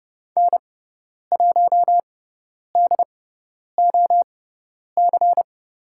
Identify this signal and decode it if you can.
Morse code, heard as a keyed tone, about 30 wpm, spaced as for 10 wpm, 700 Hz. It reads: N1DOC
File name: Lesson 18g - Letter m - Call Signs 30wpm